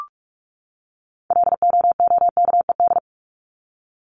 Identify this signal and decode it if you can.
Morse code, heard as a keyed tone, about 44 wpm, 700 Hz. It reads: LOOKED
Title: ED Suffix - Top 100 Words with Base Words 45wpm